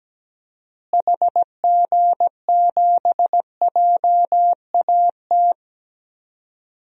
Morse code, heard as a keyed tone, about 17 wpm, 700 Hz. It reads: HG7JAT
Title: DXCC-100 Call Signs - 6 Characters - Repeated 2x - 17wpm